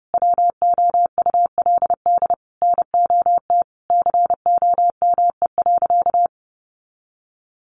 Morse code, heard as a keyed tone, about 30 wpm, 700 Hz. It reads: WOULD NOT COME.
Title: Sentences from Top 1500 Words 30wpm